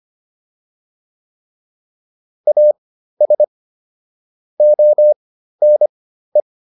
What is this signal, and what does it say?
Morse code, heard as a keyed tone, about 25 wpm, spaced as for 13 wpm, 600 Hz. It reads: AS ONE
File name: WD302 13